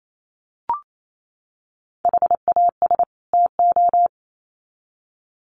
Morse code, heard as a keyed tone, about 28 wpm, 700 Hz. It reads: HAS TO
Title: Sets of 2 Words 28wpm